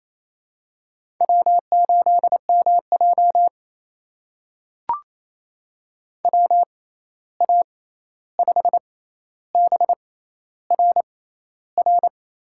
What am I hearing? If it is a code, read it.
Morse code, heard as a keyed tone, about 28 wpm, spaced as for 13 wpm, 700 Hz. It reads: W8MJ WA5BRR
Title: USA Call Signs - 6x Character Spacing - 28wpm